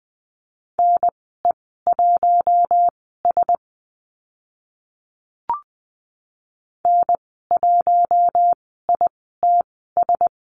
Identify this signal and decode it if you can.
Morse code, heard as a keyed tone, about 20 wpm, spaced as for 14 wpm, 700 Hz. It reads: NE1S N1ITS
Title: 042 Lesson 8g - Number 1 - Call Signs 20wpm